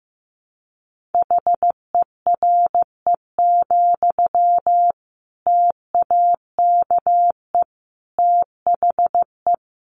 Morse code, heard as a keyed tone, about 15 wpm, 700 Hz. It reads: HERE, TAKE THE
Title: Sentences from Top 800 Words 15wpm